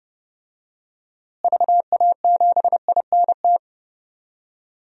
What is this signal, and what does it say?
Morse code, heard as a keyed tone, about 32 wpm, 700 Hz. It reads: VA7INT
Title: DXCC-100 Call Signs - 6 Characters - Repeated 3x - 30wpm